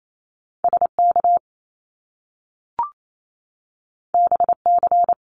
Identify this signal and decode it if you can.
Morse code, heard as a keyed tone, about 28 wpm, 700 Hz. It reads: SK BC
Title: Canadian Provinces and Territories 28wpm